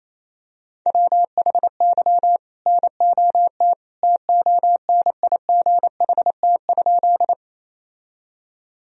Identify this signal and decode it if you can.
Morse code, heard as a keyed tone, about 28 wpm, 700 Hz. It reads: WHY NOT TONIGHT?
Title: Sentences from Top 1000 Words - Review Part 3 - 28wpm